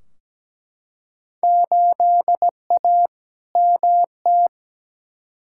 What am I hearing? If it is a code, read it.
Morse code, heard as a keyed tone, about 17 wpm, 700 Hz. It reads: 8A MT
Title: Field Day Exchanges Encore - 17wpm